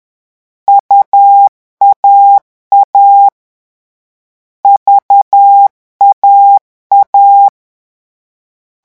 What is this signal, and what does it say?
Morse code, heard as a keyed tone, about 10 wpm, 800 Hz. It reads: UAA VAA
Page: Leçon 3 - vitesse 10 mots minute